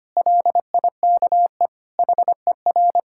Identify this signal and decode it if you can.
Morse code, heard as a keyed tone, about 25 wpm, 700 Hz. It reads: LIKE HER
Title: Sentences from Top 1000 Words - Review Part 5 - 25wpm